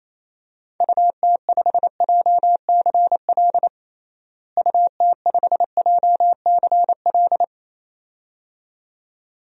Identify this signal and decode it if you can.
Morse code, heard as a keyed tone, about 28 wpm, 700 Hz. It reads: UT5JCL UT5JCL
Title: DXCC-100 Call Signs - 6 Characters - Repeated 2x - 28wpm